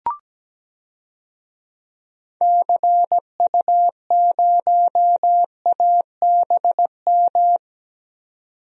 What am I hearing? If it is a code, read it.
Morse code, heard as a keyed tone, about 17 wpm, 700 Hz. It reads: CU0ABM
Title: DXCC-100 Call Signs - 6 Characters - 17wpm